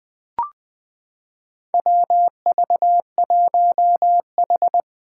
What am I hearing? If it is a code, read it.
Morse code, heard as a keyed tone, about 20 wpm, 700 Hz. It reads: WV1H
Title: Four Challenging Letter-Number Combinations 20wpm